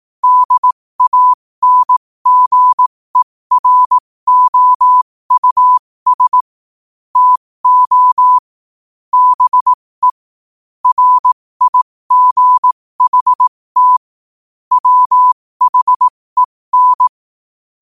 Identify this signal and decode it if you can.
Morse code, heard as a keyed tone, about 18 wpm, spaced as for 15 wpm, 1000 Hz. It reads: DANGEROUS TO BE RIGHT WHEN
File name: qotdfiles / 15wpm